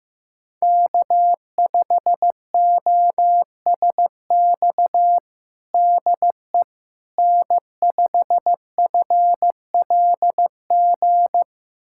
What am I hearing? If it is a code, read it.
Morse code, heard as a keyed tone, about 15 wpm, 700 Hz. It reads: K5OSX DE N5FLG
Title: USA Callsign DE Callsign - 5 Characters - 15wpm